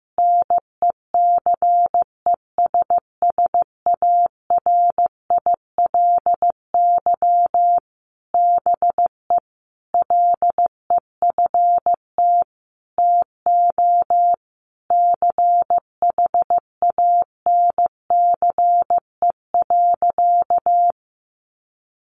Morse code, heard as a keyed tone, about 15 wpm, 700 Hz. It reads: NECESSARILY BE LEFT TO CHANCE.
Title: Sentences from Top 1400 Words 15wpm